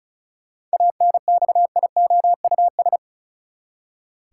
Morse code, heard as a keyed tone, about 34 wpm, 700 Hz. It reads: ANXIOUS
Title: Seven Letter Words - 35wpm